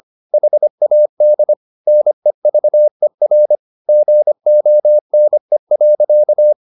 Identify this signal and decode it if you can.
Morse code, heard as a keyed tone, about 25 wpm, 600 Hz. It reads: HAD NEVER GONE.
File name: the old man and the sea part 1 25wpm